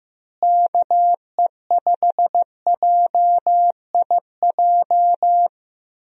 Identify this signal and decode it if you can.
Morse code, heard as a keyed tone, about 15 wpm, 700 Hz. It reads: KE5JIJ
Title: USA Call Signs - 6 Characters - Repeated 3x - 15wpm